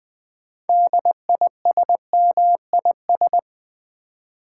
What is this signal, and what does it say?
Morse code, heard as a keyed tone, about 20 wpm, 700 Hz. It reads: DISMIS
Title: Build A Word - 3 Letters At A Time - 9-Letter Words - 20wpm